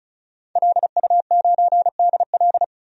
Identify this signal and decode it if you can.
Morse code, heard as a keyed tone, about 36 wpm, 700 Hz. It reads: LU9DL
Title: DXCC-100 Call Signs - 5 Characters - Repeated 2x - 35wpm